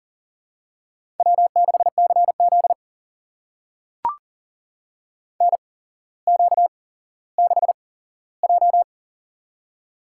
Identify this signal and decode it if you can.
Morse code, heard as a keyed tone, about 39 wpm, spaced as for 17 wpm, 700 Hz. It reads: W6CZ NQ6J
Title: USA Call Signs - 4 Characters - 8x Character Spacing - 40wpm